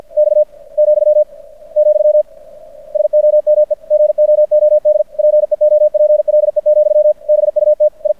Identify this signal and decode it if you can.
Morse code, heard as a keyed tone, about 53 wpm, 600 Hz. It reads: X , , SOME GOOD PEOPLE, BUT I
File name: archived / CRSnet